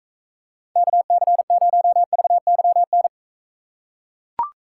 Morse code, heard as a keyed tone, about 40 wpm, 700 Hz. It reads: KC0VYN